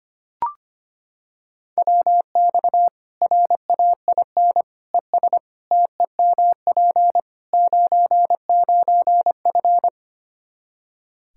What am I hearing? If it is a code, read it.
Morse code, heard as a keyed tone, about 25 wpm, 700 Hz. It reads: WX RAIN ES TEMP 99F
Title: Practice QSO - Exchange 2 - Info - 25wpm